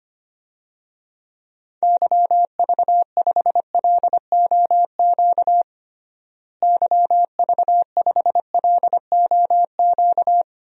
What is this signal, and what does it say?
Morse code, heard as a keyed tone, about 25 wpm, 700 Hz. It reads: YV5LOQ YV5LOQ
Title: DXCC-100 Call Signs - 6 Characters - Repeated 2x - 25wpm